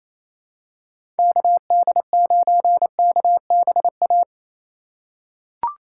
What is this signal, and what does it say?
Morse code, heard as a keyed tone, about 28 wpm, 700 Hz. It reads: KD9KBA
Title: USA Call Signs - 6 Characters - 8x Character Spacing - 28wpm